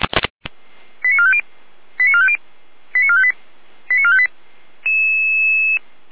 Five-tone selective call (ZVEI), heard as quick runs of five tones, tones of about 80 ms. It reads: 89450 89450 89458 89458